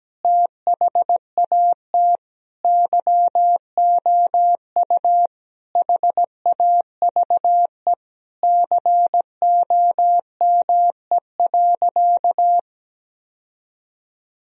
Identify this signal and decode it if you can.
Morse code, heard as a keyed tone, about 17 wpm, 700 Hz. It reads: THAT YOU HAVE COME.
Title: Sentences from Top 1000 Words - Review Part 2 - 17wpm